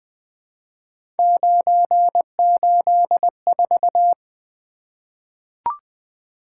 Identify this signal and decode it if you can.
Morse code, heard as a keyed tone, about 20 wpm, 700 Hz. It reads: 984
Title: Three Letter-Number Combinations - 20wpm